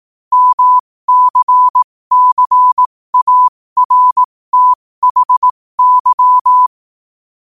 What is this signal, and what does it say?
Morse code, heard as a keyed tone, about 18 wpm, spaced as for 15 wpm, 1000 Hz. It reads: MCCARTHY
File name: qotdfiles / 15wpm